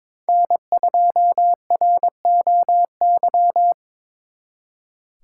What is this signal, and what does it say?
Morse code, heard as a keyed tone, about 25 wpm, 700 Hz. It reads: N2ROY
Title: Speed Racing - USA 1x3 Call Signs 22wpm